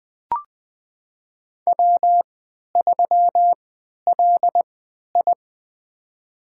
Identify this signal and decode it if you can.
Morse code, heard as a keyed tone, about 20 wpm, spaced as for 11 wpm, 700 Hz. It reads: W3LI
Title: USA Call Signs - 3x Character Spacing - 20wpm